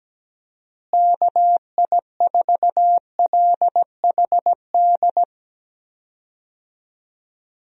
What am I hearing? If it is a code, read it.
Morse code, heard as a keyed tone, about 17 wpm, 700 Hz. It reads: KI4LHD
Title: USA Call Signs - 6 Characters - Repeated 2x - 17wpm